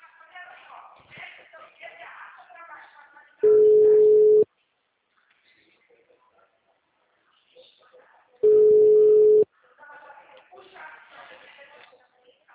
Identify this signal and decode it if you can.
Ringback tone (Europe, 425 Hz), with speech.
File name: out- -2017- - - .74266